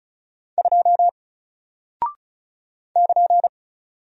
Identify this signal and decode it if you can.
Morse code, heard as a keyed tone, about 36 wpm, 700 Hz. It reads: 2 <KN>
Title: Single Character (Letter:Number:Punctuation:Prosign) Alt 35wpm